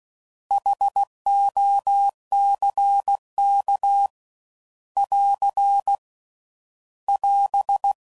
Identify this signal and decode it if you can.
Morse code, heard as a keyed tone, about 16 wpm, 800 Hz. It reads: HOCK <AR> <AS>
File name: Prefixes / By First Letter / Hs